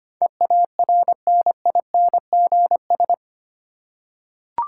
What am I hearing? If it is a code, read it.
Morse code, heard as a keyed tone, about 25 wpm, 700 Hz. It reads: EARNINGS
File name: 8 Letter Words - First 3 Letters Given - 25wpm